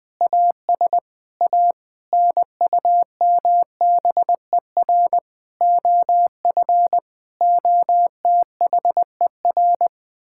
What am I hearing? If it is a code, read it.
Morse code, heard as a keyed tone, about 20 wpm, 700 Hz. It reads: AS A NUMBER OF OTHER
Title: Sentences with 7-Word Phrases - Spoken Beginning - 20wpm